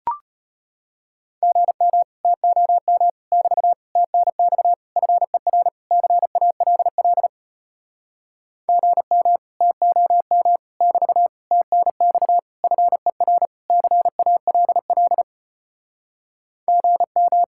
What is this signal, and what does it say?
Morse code, heard as a keyed tone, about 33 wpm, 700 Hz. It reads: GM TOM <BT> TNX FER CALL GM TOM <BT> TNX FER CALL GM